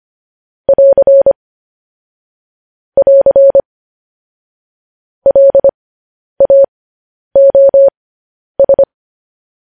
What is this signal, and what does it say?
Morse code, heard as a keyed tone, about 25 wpm, spaced as for 10 wpm, 550 Hz. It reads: <AR> <AR> LAOS